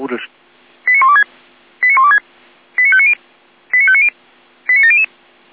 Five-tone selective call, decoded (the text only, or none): 89126 89126 89590 89590 89700